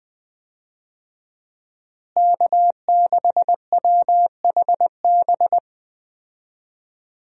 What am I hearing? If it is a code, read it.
Morse code, heard as a keyed tone, about 20 wpm, 700 Hz. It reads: K6WHB
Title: USA Call Signs - 5 Characters - Repeated 2x - 20wpm